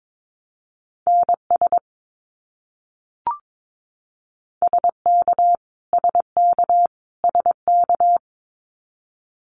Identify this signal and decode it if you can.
Morse code, heard as a keyed tone, about 22 wpm, 700 Hz. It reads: NS SK SK SK
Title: Canadian Provinces and Territories - Repeated 3x - 22wpm